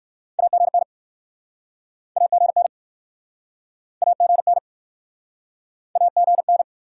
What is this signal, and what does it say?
Morse code, heard as a keyed tone, about 63 wpm, 700 Hz. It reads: AGN AGN AGN AGN